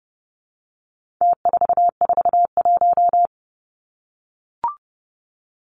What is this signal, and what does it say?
Morse code, heard as a keyed tone, about 30 wpm, 700 Hz. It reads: T441
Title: Numbers - Contest Style - 4-Digit 30wpm